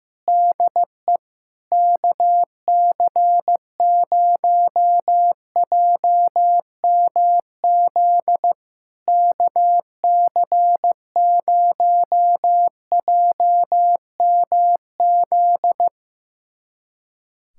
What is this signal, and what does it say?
Morse code, heard as a keyed tone, about 15 wpm, 700 Hz. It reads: DE KC0JMZ KC0JMZ
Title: SOTA QSO Practice 15wpm